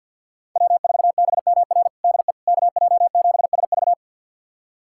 Morse code, heard as a keyed tone, about 51 wpm, 700 Hz. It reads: W4BMR DE KJ7SV
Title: USA Callsign DE Callsign - 5 Characters - 50wpm